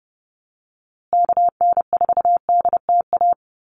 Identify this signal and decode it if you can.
Morse code, heard as a keyed tone, about 30 wpm, 700 Hz. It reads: KN4DTA
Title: USA Call Signs Repeated 2x 30wpm